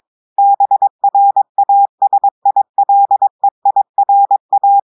Morse code, heard as a keyed tone, about 22 wpm, 800 Hz. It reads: BRASILEIRA
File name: audios / portugais / portugais texte en clair 1 vitesse 22 mots minutes